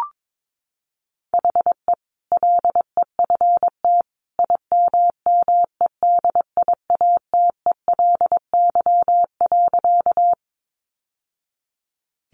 Morse code, heard as a keyed tone, about 22 wpm, 700 Hz. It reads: HE LEFT IMMEDIATELY.
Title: Sentences from Top 1100 Words 22wpm